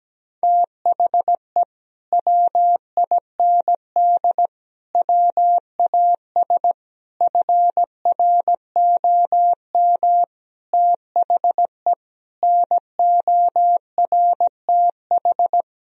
Morse code, heard as a keyed tone, about 17 wpm, 700 Hz. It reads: THE WIND WAS FROM THE NORTH
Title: Sentences from Top 1000 Words - Review Part 2 - 17wpm